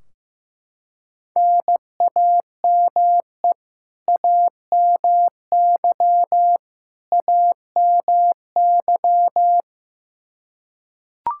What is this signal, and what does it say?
Morse code, heard as a keyed tone, about 15 wpm, 700 Hz. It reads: NAME AMY AMY
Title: Practice QSO - Exchange 1 - Beginning - 15wpm